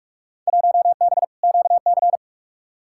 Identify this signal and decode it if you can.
Morse code, heard as a keyed tone, about 46 wpm, 700 Hz. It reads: 1B QC
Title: Field Day Exchanges Encore - 45wpm